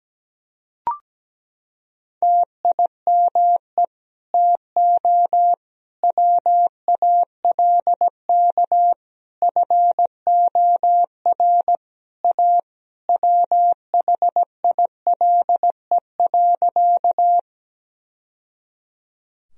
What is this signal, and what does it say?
Morse code, heard as a keyed tone, about 17 wpm, 700 Hz. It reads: TIME TO WALK FOR A WHILE.
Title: Sentences from Top 1000 Words - Review Part 1 - 17wpm